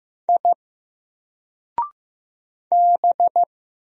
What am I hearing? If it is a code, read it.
Morse code, heard as a keyed tone, about 15 wpm, 700 Hz. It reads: I B
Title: Single Letter-Number 15wpm